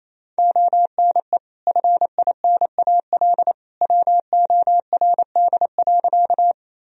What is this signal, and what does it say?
Morse code, heard as a keyed tone, about 28 wpm, 700 Hz. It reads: ONE FINAL WORD.
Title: Sentences from Top 1000 Words - Review Part 2 - 28wpm